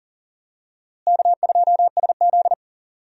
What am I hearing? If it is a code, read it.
Morse code, heard as a keyed tone, about 40 wpm, 700 Hz. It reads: K2SZ
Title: USA Call Signs - 4 Characters - Repeated 2x - 40wpm